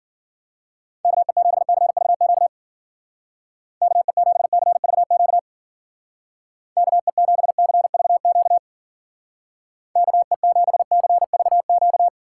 Morse code, heard as a keyed tone, about 48 wpm, 700 Hz. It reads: KE7CVQ KE7CVQ KE7CVQ KE7CVQ
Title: Speed Racing - USA 2x3 Call Signs 40wpm